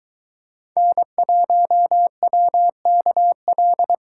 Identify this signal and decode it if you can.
Morse code, heard as a keyed tone, about 21 wpm, 700 Hz. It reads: N1WKL